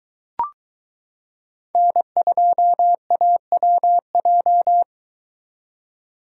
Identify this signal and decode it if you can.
Morse code, heard as a keyed tone, about 22 wpm, 700 Hz. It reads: N2AWJ